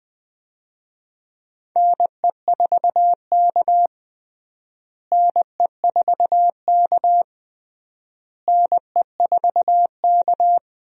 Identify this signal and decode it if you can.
Morse code, heard as a keyed tone, about 20 wpm, 700 Hz. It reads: NE4K NE4K NE4K
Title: USA Call Signs - 4 Characters - Repeated 3x - 20wpm